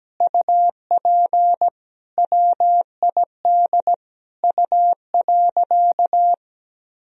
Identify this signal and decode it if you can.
Morse code, heard as a keyed tone, about 17 wpm, 700 Hz. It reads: UP WID U.
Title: Sentences from Top 300 Words and Abbreviations 17wpm